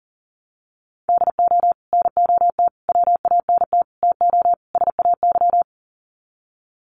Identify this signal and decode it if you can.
Morse code, heard as a keyed tone, about 40 wpm, 700 Hz. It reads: DO NOT WANT TO SAY
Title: Sets of 5 Words Encore 40wpm